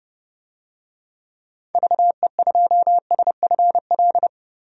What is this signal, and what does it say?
Morse code, heard as a keyed tone, about 30 wpm, 700 Hz. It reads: VE2SFL
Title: DXCC-100 Call Signs - 6 Characters - Repeated 3x - 30wpm